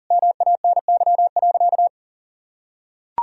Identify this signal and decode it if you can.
Morse code, heard as a keyed tone, about 40 wpm, 700 Hz. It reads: MANY.
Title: Sentences from Top 1000 Words - Review Part 1 - 40wpm